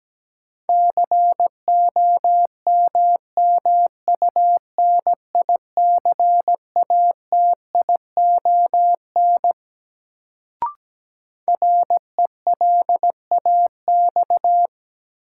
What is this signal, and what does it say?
Morse code, heard as a keyed tone, about 17 wpm, 700 Hz. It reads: COMMUNICATION RELAX
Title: Sentences from Top 1400 Words - New Word Drill Alt - 17wpm